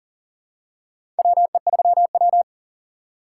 Morse code, heard as a keyed tone, about 37 wpm, 700 Hz. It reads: WE3W
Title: USA Call Signs - 4 Characters - 2x Character Spacing - 40wpm